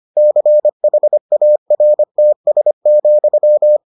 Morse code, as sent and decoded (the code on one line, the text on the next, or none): -.-. .... .- .-. - ... --..--
CHARTS,